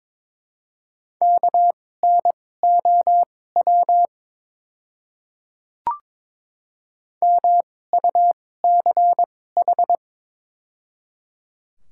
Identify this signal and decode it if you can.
Morse code, heard as a keyed tone, about 22 wpm, spaced as for 16 wpm, 700 Hz. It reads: KNOW MUCH
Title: 2x Character Spacing - Top 200 Words - 22wpm